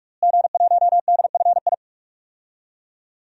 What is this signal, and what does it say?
Morse code, heard as a keyed tone, about 47 wpm, 700 Hz. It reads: G1DFI
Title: DXCC-100 Call Signs - 5 Characters - Repeated 3x - 45wpm